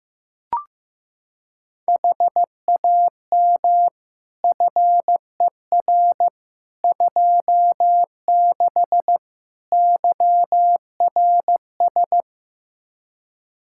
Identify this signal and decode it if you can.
Morse code, heard as a keyed tone, about 15 wpm, 700 Hz. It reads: HAM FER 26 YRS
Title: Practice QSO - Exchange 3 - More Info - 15wpm